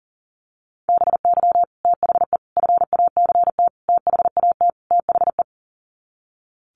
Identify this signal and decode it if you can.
Morse code, heard as a keyed tone, about 40 wpm, 700 Hz. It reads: BY THE FACT THAT THE
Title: Sets of 5 Words Encore 40wpm